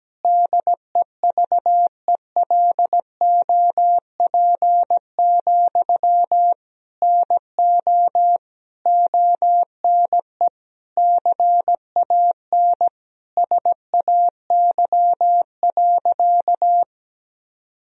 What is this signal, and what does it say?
Morse code, heard as a keyed tone, about 17 wpm, 700 Hz. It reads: DEVELOP, NO ONE CAN SAY.
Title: Sentences from Top 1000 Words - Review Part 2 - 17wpm